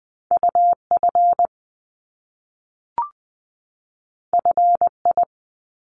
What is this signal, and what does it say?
Morse code, heard as a keyed tone, about 20 wpm, 700 Hz. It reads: UF FI
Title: Top Two Letter Combinations in Words 20wpm